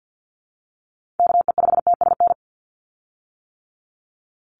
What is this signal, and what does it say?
Morse code, heard as a keyed tone, about 50 wpm, 700 Hz. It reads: KE5TSN
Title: USA Call Signs 50wpm